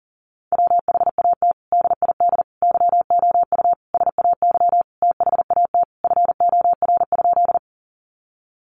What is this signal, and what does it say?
Morse code, heard as a keyed tone, about 40 wpm, 700 Hz. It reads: WHAT DID YOU SAY THAT FOR?
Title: Sentences from Top 200 Words 40wpm